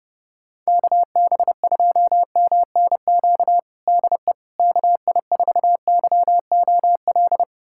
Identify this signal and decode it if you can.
Morse code, heard as a keyed tone, about 30 wpm, 700 Hz. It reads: KB2MNQ DE KI4YOL
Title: USA Callsign DE Callsign - 6 Characters - 30wpm